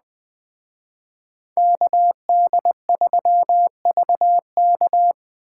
Morse code, heard as a keyed tone, about 20 wpm, 700 Hz. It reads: KD3VK
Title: Speed Racing - USA 2x2 Call Signs 17wpm